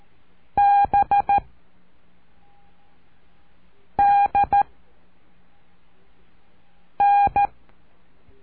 Morse code, heard as a keyed tone, about 13 wpm, 810 Hz. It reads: B D N
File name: cursos / CURSO DE TELEGRAFIA / Telegrafia - Trilha 04